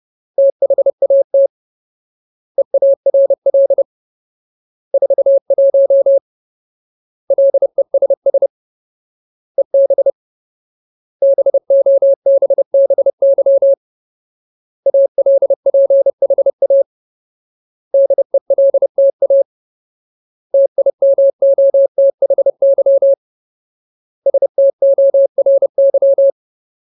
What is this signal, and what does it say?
Morse code, heard as a keyed tone, about 30 wpm, 550 Hz. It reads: THAT EARL 41 LESS EB BOBBY ALPHA DELTA TIMOTHY STORY
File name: Ultimate Word List / 30WPM Ultimate